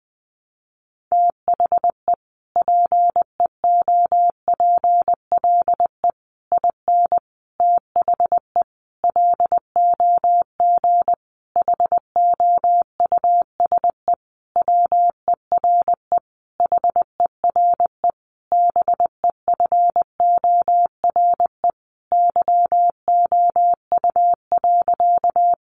Morse code, heard as a keyed tone, about 20 wpm, 700 Hz. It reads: THE PEOPLE IN THE LOG HOUSE WERE HERE BEFORE YOU.
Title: Sentences from Top 1400 Words 20wpm